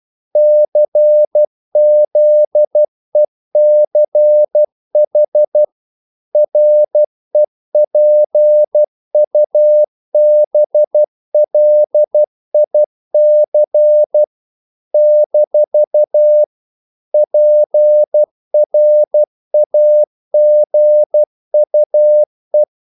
Morse code, heard as a keyed tone, about 12 wpm, 600 Hz. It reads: CZECH REPUBLIC - PRAGUE
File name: CW / Practice / NationalCapitals12